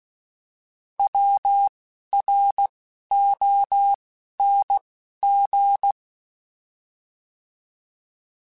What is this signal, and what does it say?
Morse code, heard as a keyed tone, about 16 wpm, spaced as for 11 wpm, 800 Hz. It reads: WRONG
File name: Misc / Misc-mp3s / Words @10